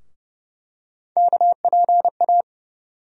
Morse code, heard as a keyed tone, about 30 wpm, 700 Hz. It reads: KPA
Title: USA Call Sign Postfixes - 3 Letters 30wpm